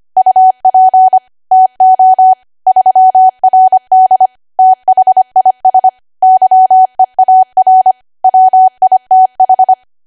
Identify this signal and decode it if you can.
Morse code, heard as a keyed tone, about 25 wpm, 750 Hz. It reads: UP TO 3RD THIS YEAR WITH